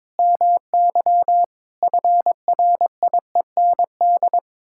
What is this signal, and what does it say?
Morse code, heard as a keyed tone, about 21 wpm, 700 Hz. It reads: MY FRIEND
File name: practice / 22wpm / Sets of 8 Words - 22wpm